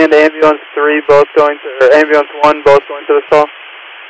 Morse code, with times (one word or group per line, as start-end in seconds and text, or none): none